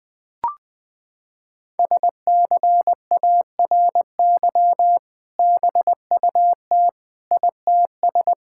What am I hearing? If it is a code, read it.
Morse code, heard as a keyed tone, about 20 wpm, 700 Hz. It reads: SCARY BUT ITS
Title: Sentences with 2-Word Phrases - 20wpm